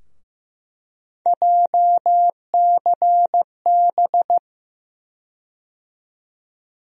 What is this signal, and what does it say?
Morse code, heard as a keyed tone, about 15 wpm, 700 Hz. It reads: JCB
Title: USA Call Sign Postfixes - 3 Letters 15wpm